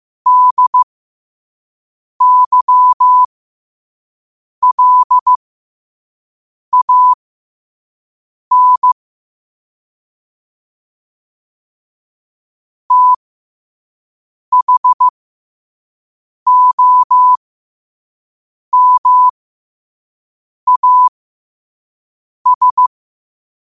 Morse code, heard as a keyed tone, about 15 wpm, spaced as for 5 wpm, 1000 Hz. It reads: DYLAN THOMAS
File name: qotdfiles / 5wpm